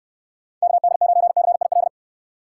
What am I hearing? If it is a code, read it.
Morse code, heard as a keyed tone, about 69 wpm, 700 Hz. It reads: KD9CIL